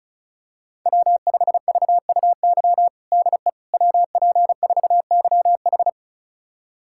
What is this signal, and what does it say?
Morse code, heard as a keyed tone, about 36 wpm, 690 Hz. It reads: W5VUY DE WP4YH
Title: USA Callsign DE Callsign - 5 Characters - 35wpm